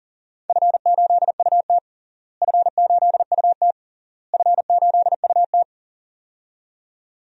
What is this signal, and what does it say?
Morse code, heard as a keyed tone, about 37 wpm, 700 Hz. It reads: F8UT F8UT F8UT
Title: DXCC-100 Call Signs - 4 Characters - Repeated 3x - 40wpm